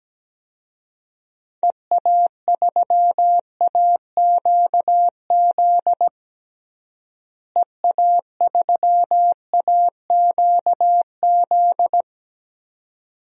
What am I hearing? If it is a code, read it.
Morse code, heard as a keyed tone, about 17 wpm, 700 Hz. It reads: EA3AQZ EA3AQZ
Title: DXCC-100 Call Signs - 6 Characters - Repeated 2x - 17wpm